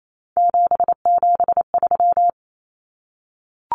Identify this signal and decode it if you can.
Morse code, heard as a keyed tone, about 28 wpm, 700 Hz. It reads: 773
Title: Common Cut Numbers - 3-Digit 28wpm